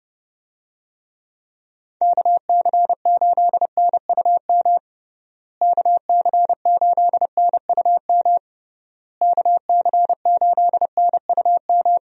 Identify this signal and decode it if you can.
Morse code, heard as a keyed tone, about 32 wpm, 700 Hz. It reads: KC8NUM KC8NUM KC8NUM
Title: USA Call Signs - 6 Characters - Repeated 3x - 30wpm